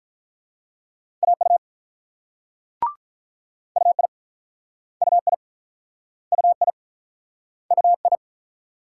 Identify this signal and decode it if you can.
Morse code, heard as a keyed tone, about 46 wpm, 700 Hz. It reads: AU UI UI UI UI